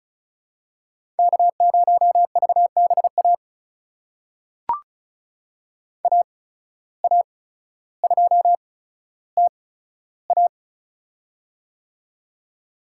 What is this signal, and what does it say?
Morse code, heard as a keyed tone, about 34 wpm, spaced as for 15 wpm, 700 Hz. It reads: K0VBA AA2TA
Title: USA Call Signs - 5 Characters - 8x Character Spacing - 35wpm